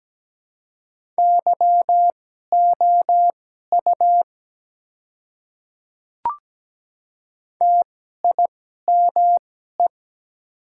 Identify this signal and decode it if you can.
Morse code, heard as a keyed tone, about 17 wpm, spaced as for 12 wpm, 690 Hz. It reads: YOU TIME
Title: 2x Character Spacing - Top 100 Words - 17wpm